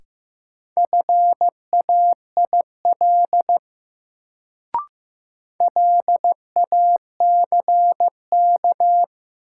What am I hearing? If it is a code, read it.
Morse code, heard as a keyed tone, about 15 wpm, 700 Hz. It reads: FAIL LACK
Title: Sentences from Top 1100 Words - New Word Drill Alt - 15wpm